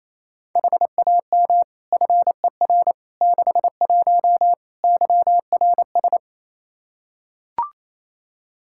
Morse code, heard as a keyed tone, about 28 wpm, 700 Hz. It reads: HAM FER 61 YRS
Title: Practice QSO - Exchange 3 - More Info - 28wpm